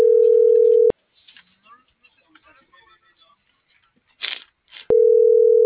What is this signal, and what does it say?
Ringback tone (North America), 440 + 480 Hz.